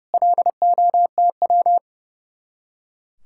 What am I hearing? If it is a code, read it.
Morse code, heard as a keyed tone, about 36 wpm, 700 Hz. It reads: LOTW
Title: Speed-Racing Common QSO Phrases Encore - 30wpm